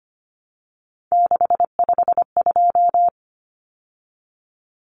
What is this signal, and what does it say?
Morse code, heard as a keyed tone, about 25 wpm, 700 Hz. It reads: 652
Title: Numbers - 3-Digit 25wpm